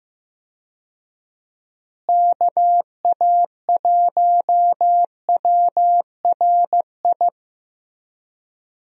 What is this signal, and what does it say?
Morse code, heard as a keyed tone, about 15 wpm, 700 Hz. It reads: KA1WRI
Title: USA Call Signs - 6 Characters - Repeated 2x - 15wpm